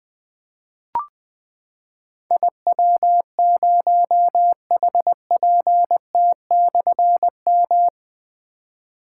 Morse code, heard as a keyed tone, about 21 wpm, 700 Hz. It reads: IW0HPT/M
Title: Difficult Call Signs 20wpm